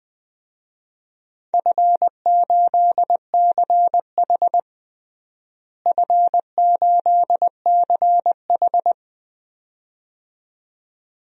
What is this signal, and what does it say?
Morse code, heard as a keyed tone, about 21 wpm, 700 Hz. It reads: F8CH F8CH
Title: DXCC-100 Call Signs - 4 Characters - Repeated 2x - 20wpm